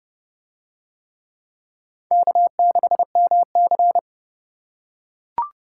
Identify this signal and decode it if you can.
Morse code, heard as a keyed tone, about 29 wpm, 700 Hz. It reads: K6MC